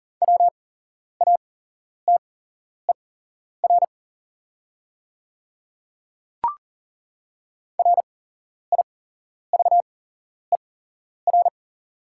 Morse code, heard as a keyed tone, about 41 wpm, spaced as for 11 wpm, 700 Hz. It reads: WATER RIVER
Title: 8x Character Spacing - Top 300 Words - 40wpm